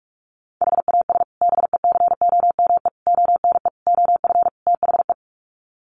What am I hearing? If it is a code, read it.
Morse code, heard as a keyed tone, about 44 wpm, 700 Hz. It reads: HAS BECOME ONE OF THE
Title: Sets of 5 Words Encore 45wpm